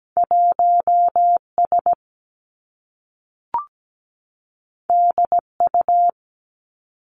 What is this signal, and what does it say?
Morse code, heard as a keyed tone, about 17 wpm, 700 Hz. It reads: 1S DU
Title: Two Challenging Letter-Number Combinations 17wpm